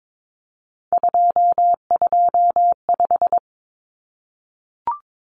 Morse code, heard as a keyed tone, about 22 wpm, 700 Hz. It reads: 225
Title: Common Cut Numbers - 3-Digit 22wpm